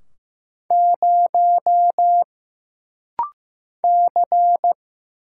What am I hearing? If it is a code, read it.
Morse code, heard as a keyed tone, about 15 wpm, 700 Hz. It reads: 0 C
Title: Single Letter-Number Alt - Rapid-Fire 15wpm